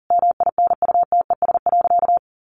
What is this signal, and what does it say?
Morse code, heard as a keyed tone, about 40 wpm, 700 Hz. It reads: MINUTES.
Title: A Sound Of Thunder 40wpm